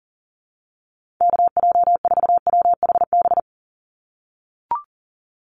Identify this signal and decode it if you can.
Morse code, heard as a keyed tone, about 40 wpm, 700 Hz. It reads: KJ4WHB
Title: USA Call Signs 40wpm